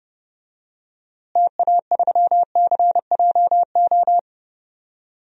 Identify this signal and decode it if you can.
Morse code, heard as a keyed tone, about 30 wpm, 700 Hz. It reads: TA3CJO
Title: DXCC-100 Call Signs - 6 Characters - Repeated 3x - 30wpm